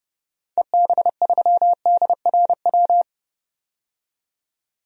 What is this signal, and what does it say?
Morse code, heard as a keyed tone, about 30 wpm, 700 Hz. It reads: EB3DRW
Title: DXCC-100 Call Signs - 6 Characters - 30wpm